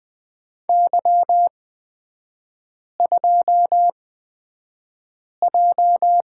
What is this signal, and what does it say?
Morse code, heard as a keyed tone, about 20 wpm, spaced as for 5 wpm, 700 Hz. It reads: Y2J
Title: 224 Lesson 36e - BT - Three Characters 20wpm